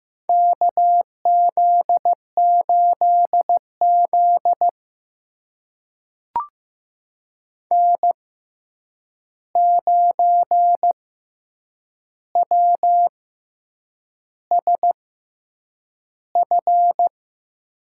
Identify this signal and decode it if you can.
Morse code, heard as a keyed tone, about 15 wpm, spaced as for 7 wpm, 700 Hz. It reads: KZ8Z N9WSF
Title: USA Call Signs - 6x Character Spacing - 15wpm